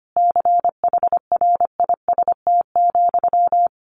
Morse code, heard as a keyed tone, about 25 wpm, 700 Hz. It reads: CHRIST,
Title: A Sound Of Thunder 25wpm